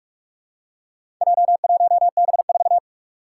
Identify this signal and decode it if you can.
Morse code, heard as a keyed tone, about 44 wpm, 700 Hz. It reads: J1B4
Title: Four Challenging Letter-Number Combinations 45wpm